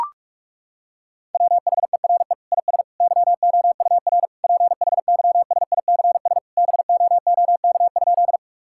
Morse code, heard as a keyed tone, about 41 wpm, 700 Hz. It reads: WHERE IS YOUR PHYSICS BOOK?
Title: Sentences from Top 1000 Words - Review Part 5 - 45wpm